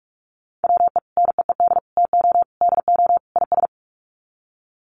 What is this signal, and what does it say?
Morse code, heard as a keyed tone, about 44 wpm, 700 Hz. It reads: WE NEED TO DO IS
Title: Sets of 5 Words Encore2 45wpm